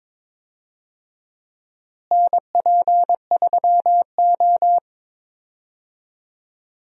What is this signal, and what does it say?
Morse code, heard as a keyed tone, about 22 wpm, 700 Hz. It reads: NP3O
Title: USA Call Signs - 4 Characters - Repeated 2x - 22wpm